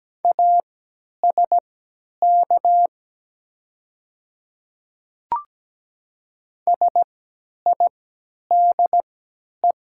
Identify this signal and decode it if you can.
Morse code, heard as a keyed tone, about 17 wpm, spaced as for 10 wpm, 700 Hz. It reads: ASK SIDE
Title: 3x Character Spacing - Top 300 Words - 17wpm